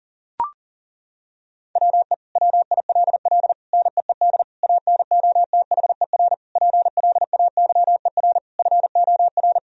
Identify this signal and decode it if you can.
Morse code, heard as a keyed tone, about 40 wpm, 700 Hz. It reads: WE WILL NEED ANOTHER PLAYER FOR
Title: Sentences from Top 1000 Words - Review Part 4 - 40wpm